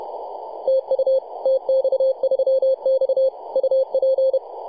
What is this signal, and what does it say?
Morse code, heard as a keyed tone, about 31 wpm, 530 Hz. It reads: TU TX3X UP
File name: posts / 2015-oct-03-2319z-tx3x-21-023mhz-cw